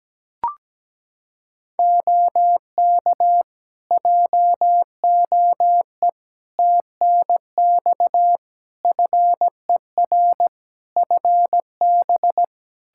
Morse code, heard as a keyed tone, about 17 wpm, 700 Hz. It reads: OK JOE TNX FER FB
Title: K1USN SST QSO Practice 17wpm